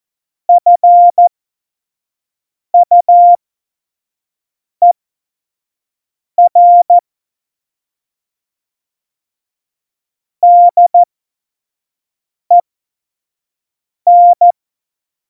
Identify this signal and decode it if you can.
Morse code, heard as a keyed tone, about 14 wpm, spaced as for 5 wpm, 700 Hz. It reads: FUER DEN
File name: Morsekurs / Witze / Witz 09. Januar 2018, 25 F70